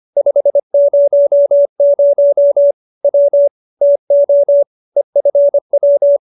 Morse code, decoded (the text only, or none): 500 W TO EFW